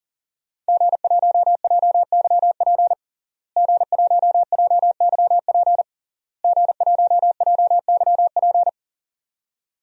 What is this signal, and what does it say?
Morse code, heard as a keyed tone, about 40 wpm, 700 Hz. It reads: G1JYP G1JYP G1JYP
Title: DXCC-100 Call Signs - 5 Characters - Repeated 3x - 40wpm